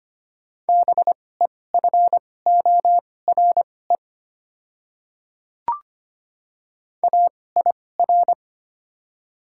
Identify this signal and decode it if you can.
Morse code, heard as a keyed tone, about 25 wpm, spaced as for 18 wpm, 700 Hz. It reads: BEFORE AIR
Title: 2x Character Spacing - Top 200 Words - 25wpm